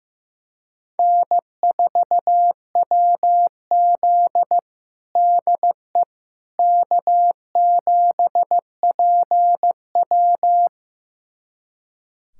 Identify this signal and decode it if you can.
Morse code, heard as a keyed tone, about 16 wpm, 700 Hz. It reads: N4WZ DE K7PW
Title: USA Callsign DE Callsign - 4 Characters - 15wpm